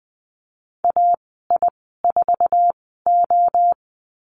Morse code, heard as a keyed tone, about 19 wpm, spaced as for 14 wpm, 700 Hz. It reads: AI4O
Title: 082 Lesson 14g - Number 2 - Call Signs 20wpm